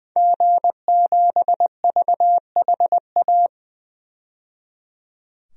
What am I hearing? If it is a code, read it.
Morse code, heard as a keyed tone, about 21 wpm, 700 Hz. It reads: G7VHA
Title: DXCC-100 Call Signs - 5 Characters - 20wpm